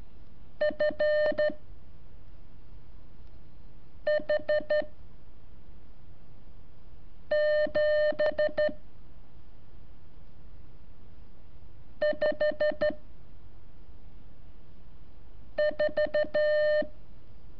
Morse code, heard as a keyed tone, about 12 wpm, 600 Hz. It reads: F H 7 5 4
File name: curso / ejercicio 22